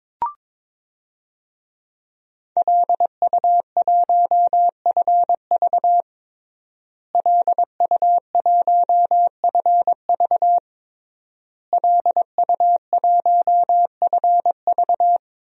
Morse code, heard as a keyed tone, about 22 wpm, 700 Hz. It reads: LU1FV LU1FV LU1FV
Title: DXCC-100 Call Signs - 5 Characters - Repeated 3x - 22wpm